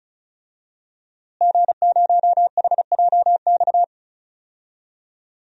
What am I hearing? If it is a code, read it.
Morse code, heard as a keyed tone, about 34 wpm, 700 Hz. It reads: G0HJX
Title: DXCC-100 Call Signs - 5 Characters - Repeated 2x - 35wpm